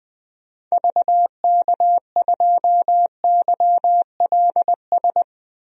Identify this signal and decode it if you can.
Morse code, heard as a keyed tone, about 20 wpm, 700 Hz. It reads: VK2YLS
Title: DXCC-100 Call Signs - 6 Characters - 20wpm